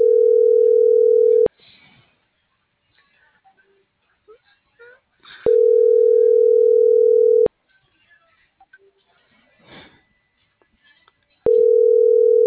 Ringback tone (North America), 440 + 480 Hz.